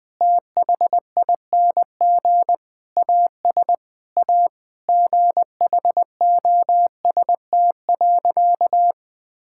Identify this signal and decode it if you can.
Morse code, heard as a keyed tone, about 21 wpm, 700 Hz. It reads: THING AS A GHOST.
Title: Sentences with 8-Word Phrases - 20wpm